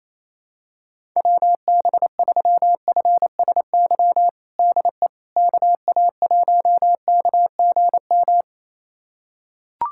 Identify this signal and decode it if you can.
Morse code, heard as a keyed tone, about 28 wpm, 700 Hz. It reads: WB3FSY DE KA1KGM
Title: USA Callsign DE Callsign - 6 Characters - 28wpm